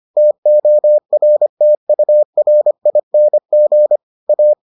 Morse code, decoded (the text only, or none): TORTURING A